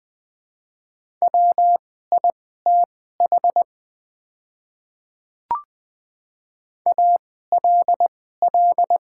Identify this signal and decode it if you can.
Morse code, heard as a keyed tone, about 20 wpm, spaced as for 14 wpm, 700 Hz. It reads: WITH ALL
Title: 2x Character Spacing - Top 100 Words - 20wpm